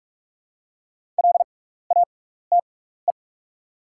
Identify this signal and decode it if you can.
Morse code, heard as a keyed tone, about 44 wpm, spaced as for 15 wpm, 700 Hz. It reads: LATE
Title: 6x Character Spacing - Top 500 Words - 45wpm